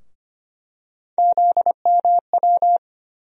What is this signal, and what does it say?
Morse code, heard as a keyed tone, about 25 wpm, 700 Hz. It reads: ZMW
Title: USA Call Sign Postfixes - 3 Letters 25wpm